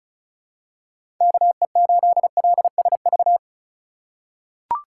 Morse code, heard as a keyed tone, about 35 wpm, 700 Hz. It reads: KE8LSV
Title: USA Call Signs - 6 Characters - 3x Character Spacing - 35wpm